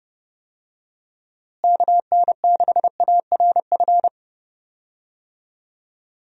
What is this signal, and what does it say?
Morse code, heard as a keyed tone, about 31 wpm, 700 Hz. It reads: KN6ARF
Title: USA Call Signs - 6 Characters - Repeated 3x - 30wpm